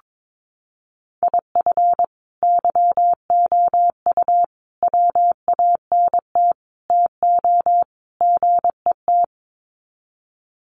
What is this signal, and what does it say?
Morse code, heard as a keyed tone, about 22 wpm, 700 Hz. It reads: IF YOU WANT TO GET
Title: Sets of 5 Words Encore 22wpm